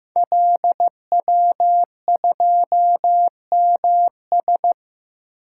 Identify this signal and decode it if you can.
Morse code, heard as a keyed tone, about 15 wpm, 700 Hz. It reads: LW2MS
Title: Code Groups - 5 Letter:Numbers - 15wpm